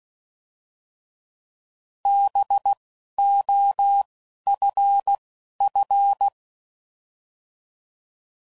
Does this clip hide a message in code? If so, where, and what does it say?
Yes: Morse code from 2.1 to 6.3 s, heard as a keyed tone, about 16 wpm, 800 Hz. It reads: B O F F